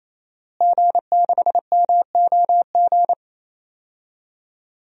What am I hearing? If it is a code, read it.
Morse code, heard as a keyed tone, about 28 wpm, 700 Hz. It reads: G6MOG
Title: DXCC-100 Call Signs - 5 Characters - Repeated 3x - 28wpm